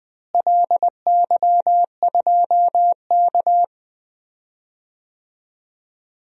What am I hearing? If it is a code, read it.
Morse code, heard as a keyed tone, about 20 wpm, 700 Hz. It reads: LY2K
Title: DXCC-100 Call Signs - 4 Characters - Repeated 3x - 20wpm